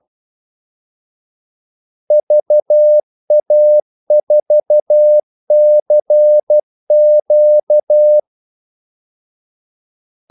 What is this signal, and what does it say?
Morse code, heard as a keyed tone, about 12 wpm, 600 Hz. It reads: VA4CQ
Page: 2 ekstra ordmellemrum mellem kaldesignalerne